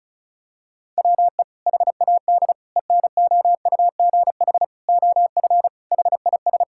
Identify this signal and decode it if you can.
Morse code, heard as a keyed tone, about 34 wpm, 700 Hz. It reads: WE HAD ENOUGH OF HIS
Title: Sentences from Top 1600 Words 35wpm